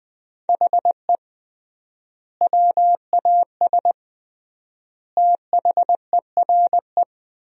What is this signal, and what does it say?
Morse code, heard as a keyed tone, about 21 wpm, 700 Hz. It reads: HE WAS THERE
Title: 3x Word-Spacing - Sentences from Top 200 Words - 20wpm